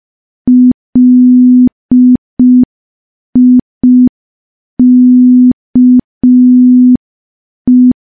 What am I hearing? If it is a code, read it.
Morse code, heard as a keyed tone, about 5 wpm, 250 Hz. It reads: LIKE
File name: morse / morse-training-05WPM